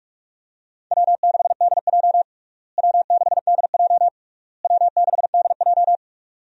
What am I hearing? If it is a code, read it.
Morse code, heard as a keyed tone, about 46 wpm, 700 Hz. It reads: W6DJ W6DJ W6DJ
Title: USA Call Signs - 4 Characters - Repeated 3x - 45wpm